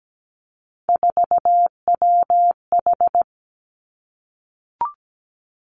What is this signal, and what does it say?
Morse code, heard as a keyed tone, about 17 wpm, 700 Hz. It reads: 4WH
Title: Three Challenging Letter-Number Combinations 17wpm